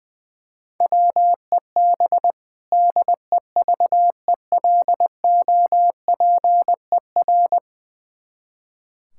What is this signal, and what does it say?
Morse code, heard as a keyed tone, about 21 wpm, 700 Hz. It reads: WEB DEVELOPER
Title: K1USN SST QSO Practice 20wpm